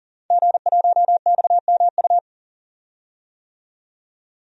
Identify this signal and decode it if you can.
Morse code, heard as a keyed tone, about 40 wpm, 700 Hz. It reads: G1XMU
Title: DXCC-100 Call Signs - 5 Characters - 40wpm